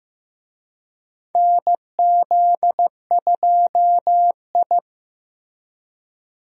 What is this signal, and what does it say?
Morse code, heard as a keyed tone, about 15 wpm, 700 Hz. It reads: NZ2I
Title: USA Call Signs - 4 Characters - Repeated 2x - 15wpm